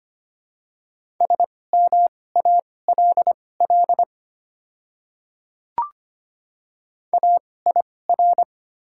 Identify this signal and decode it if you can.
Morse code, heard as a keyed tone, about 25 wpm, spaced as for 18 wpm, 700 Hz. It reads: SMALL AIR
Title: 2x Character Spacing - Top 200 Words - 25wpm